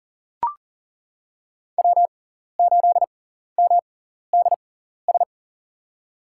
Morse code, heard as a keyed tone, about 40 wpm, spaced as for 14 wpm, 690 Hz. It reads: W8MDS
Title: USA Call Signs - 5 Characters - 6x Character Spacing - 40wpm